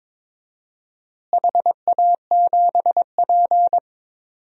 Morse code, heard as a keyed tone, about 22 wpm, 700 Hz. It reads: HA7P
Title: DXCC-100 Call Signs - 4 Characters - Repeated 3x - 22wpm